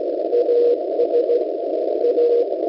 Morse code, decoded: AS A